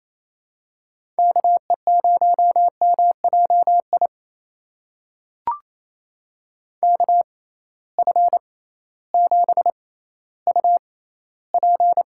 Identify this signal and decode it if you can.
Morse code, heard as a keyed tone, about 28 wpm, spaced as for 15 wpm, 700 Hz. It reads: KE0MJI KF7UP
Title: USA Call Signs - 6 Characters - 6x Character Spacing - 28wpm